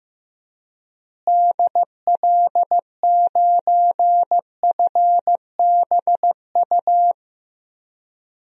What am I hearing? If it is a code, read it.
Morse code, heard as a keyed tone, about 15 wpm, 700 Hz. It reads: DL9FBU
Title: DXCC-100 Call Signs - 6 Characters - Repeated 3x - 15wpm